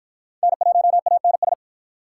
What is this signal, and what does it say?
Morse code, heard as a keyed tone, about 48 wpm, 700 Hz. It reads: N1ANS